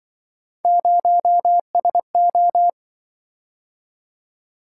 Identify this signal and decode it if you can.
Morse code, heard as a keyed tone, about 26 wpm, 700 Hz. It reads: 0SO